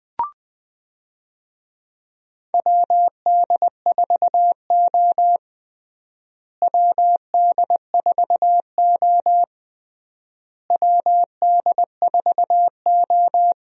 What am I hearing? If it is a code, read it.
Morse code, heard as a keyed tone, about 20 wpm, 700 Hz. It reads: WD4O WD4O WD4O
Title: USA Call Signs - 4 Characters - Repeated 3x - 20wpm